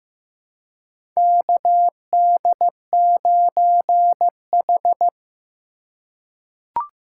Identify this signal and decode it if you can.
Morse code, heard as a keyed tone, about 15 wpm, 700 Hz. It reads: KD9H
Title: USA Call Signs - 4x Character Spacing - 15wpm